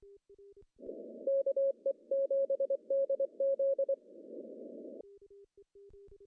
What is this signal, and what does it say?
Morse code, heard as a keyed tone, about 23 wpm, 550 Hz. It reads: KE7DZ
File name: CW / cqww-cw-k5tr